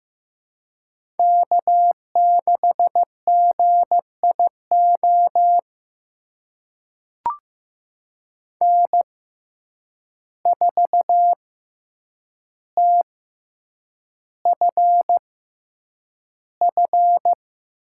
Morse code, heard as a keyed tone, about 15 wpm, spaced as for 8 wpm, 700 Hz. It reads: K6GIO N4TFF
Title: USA Call Signs - 5 Characters - 6x Character Spacing - 15wpm